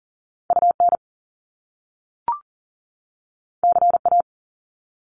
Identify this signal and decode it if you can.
Morse code, heard as a keyed tone, about 38 wpm, 700 Hz. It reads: UN CA
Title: Top Two Letter Combinations in Words 40wpm